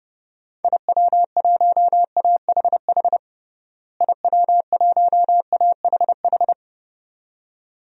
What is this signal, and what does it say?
Morse code, heard as a keyed tone, about 30 wpm, 700 Hz. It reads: IW1AHH IW1AHH
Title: DXCC-100 Call Signs - 6 Characters - Repeated 2x - 30wpm